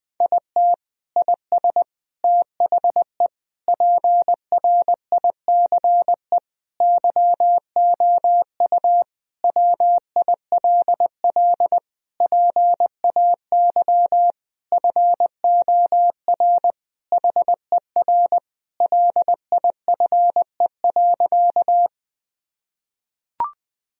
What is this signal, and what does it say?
Morse code, heard as a keyed tone, about 21 wpm, 700 Hz. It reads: IT IS THE PRICE YOU WILL PAY FOR HER LIFE.
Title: Sentences from Top 1000 Words - Review Part 4 - 20wpm